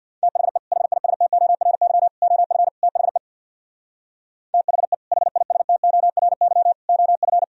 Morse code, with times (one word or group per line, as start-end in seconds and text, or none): 0.2-0.6 THE
0.7-2.1 HISTORY
2.2-2.7 OF
2.8-3.2 THE
4.5-4.9 THE
5.1-6.7 HISTORY
6.9-7.4 OF